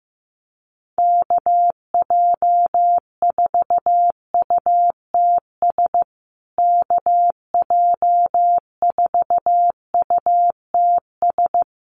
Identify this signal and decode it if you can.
Morse code, heard as a keyed tone, about 15 wpm, 700 Hz. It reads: KJ4UTS KJ4UTS
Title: USA Call Signs Repeated 3x 15wpm